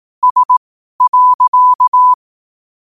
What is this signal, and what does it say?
Morse code, heard as a keyed tone, about 18 wpm, spaced as for 12 wpm, 1000 Hz. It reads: S.
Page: Quotes for Thu, 14 Aug 2025 in Morse Code at 12 words per minute.